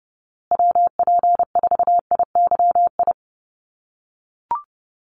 Morse code, heard as a keyed tone, about 30 wpm, 700 Hz. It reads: WP4IYI
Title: USA Call Signs Repeated 3x 30wpm